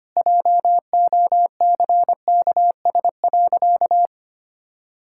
Morse code, heard as a keyed tone, about 25 wpm, 700 Hz. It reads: JOCKS.
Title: Pangram Challenge v2 - 25wpm